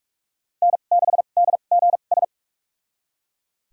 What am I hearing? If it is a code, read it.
Morse code, heard as a keyed tone, about 41 wpm, spaced as for 31 wpm, 700 Hz. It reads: N6DGS
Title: USA Call Signs - 5 Characters - 2x Character Spacing - 45wpm